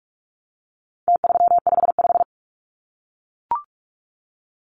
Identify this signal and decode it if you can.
Morse code, heard as a keyed tone, about 44 wpm, 700 Hz. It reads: T355
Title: Numbers - Contest Style - 4-Digit 45wpm